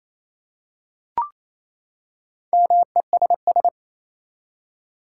Morse code, heard as a keyed tone, about 27 wpm, 700 Hz. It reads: MESS
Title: Four Letter Words - 28wpm